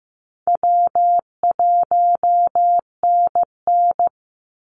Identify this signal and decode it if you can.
Morse code, heard as a keyed tone, about 15 wpm, 700 Hz. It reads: W1NN
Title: Call Signs Repeated 3x - 15wpm